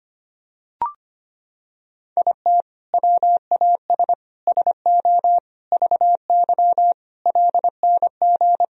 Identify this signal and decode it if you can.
Morse code, heard as a keyed tone, about 25 wpm, 700 Hz. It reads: IT WAS SO VY LNG
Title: Sentences from Top 500 Words and Abbreviations 25wpm